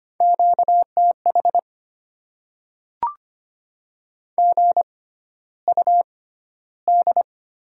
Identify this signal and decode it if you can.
Morse code, heard as a keyed tone, about 25 wpm, spaced as for 13 wpm, 700 Hz. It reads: QTH GUD
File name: Top 20 QSO Elements - 6x Character Spacing - 25wpm